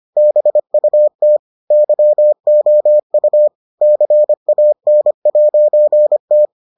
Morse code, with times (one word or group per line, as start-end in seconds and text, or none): 0.2-1.4 BUT
1.7-3.5 YOU
3.8-6.5 CAN'T